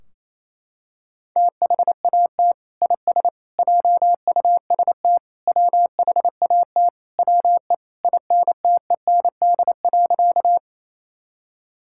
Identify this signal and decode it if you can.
Morse code, heard as a keyed tone, about 28 wpm, 700 Hz. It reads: THAT IS JUST WHAT WE INTEND.
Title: Sentences from Top 1600 Words 28wpm